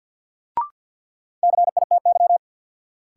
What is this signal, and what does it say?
Morse code, heard as a keyed tone, about 49 wpm, 700 Hz. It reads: CITY
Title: Sets of 2 Words - New Word Drill - 50wpm